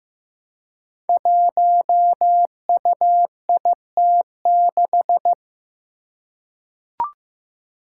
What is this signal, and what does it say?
Morse code, heard as a keyed tone, about 15 wpm, 700 Hz. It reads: 1UIT6
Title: Five Challenging Letter-Number Combinations 15wpm